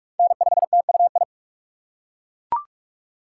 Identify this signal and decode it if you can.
Morse code, heard as a keyed tone, about 42 wpm, 700 Hz. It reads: N5TUI
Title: USA Call Signs - 5 Characters - 2x Character Spacing - 45wpm